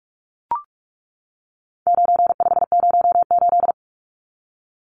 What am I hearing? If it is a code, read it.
Morse code, heard as a keyed tone, about 44 wpm, 700 Hz. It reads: 9508
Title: Numbers - 4-Digit 45wpm